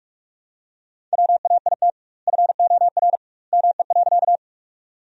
Morse code, heard as a keyed tone, about 41 wpm, 700 Hz. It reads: WAIT FOR ME.
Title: 2x Word-Spacing - Sentences from Top 500 Words - 45wpm